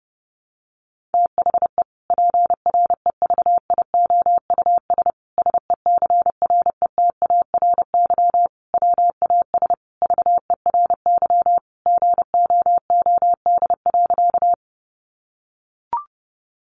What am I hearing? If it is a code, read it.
Morse code, heard as a keyed tone, about 28 wpm, 700 Hz. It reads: THE PREVIOUS SECRETARY WAS VERY GOOD.
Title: Sentences from Top 1500 Words 30wpm